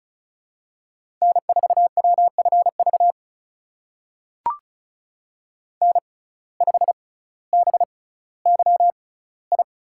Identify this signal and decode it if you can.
Morse code, heard as a keyed tone, about 36 wpm, spaced as for 18 wpm, 700 Hz. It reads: N4WFV N5BYI
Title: USA Call Signs - 5 Characters - 6x Character Spacing - 35wpm